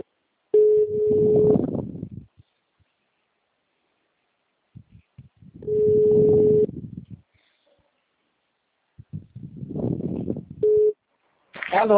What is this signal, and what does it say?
Ringback tone (Europe, 425 Hz), with speech.